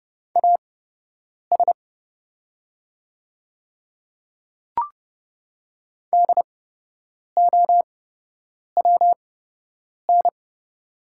Morse code, heard as a keyed tone, about 30 wpm, spaced as for 8 wpm, 700 Hz. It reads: AS DOWN
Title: 8x Character Spacing - Top 100 Words - 30wpm